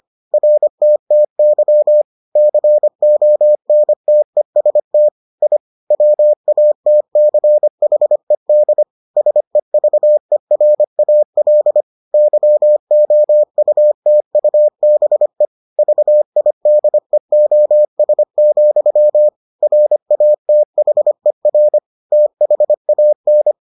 Morse code, heard as a keyed tone, about 25 wpm, 600 Hz. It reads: RTTY CONTEST I WATCHED SEVERAL YOUTUBE VIDEOS, RATHER THAN